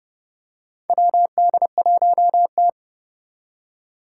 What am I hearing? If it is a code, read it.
Morse code, heard as a keyed tone, about 30 wpm, 700 Hz. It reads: WD1T
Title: Four Challenging Letter-Number Combinations 30wpm